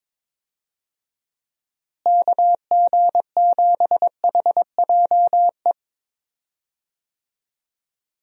Morse code, heard as a keyed tone, about 22 wpm, 700 Hz. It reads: KG7HJE
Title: USA Call Signs - 6 Characters - Repeated 3x - 22wpm